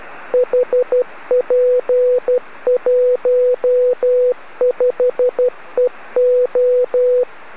Morse code, heard as a keyed tone, about 12 wpm, 500 Hz. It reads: HP15EO
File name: OX3SIX B